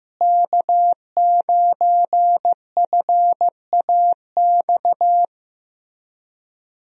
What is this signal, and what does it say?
Morse code, heard as a keyed tone, about 15 wpm, 700 Hz. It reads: K9FAX
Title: USA Call Signs - 5 Characters - Repeated 3x - 15wpm